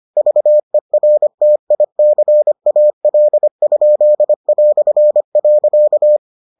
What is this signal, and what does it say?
Morse code, heard as a keyed tone, about 25 wpm, 600 Hz. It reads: VERTICAL?".